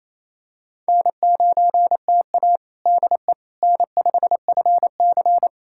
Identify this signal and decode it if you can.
Morse code, heard as a keyed tone, about 28 wpm, 700 Hz. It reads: N9TA DE N5FC
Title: USA Callsign DE Callsign - 4 Characters - 28wpm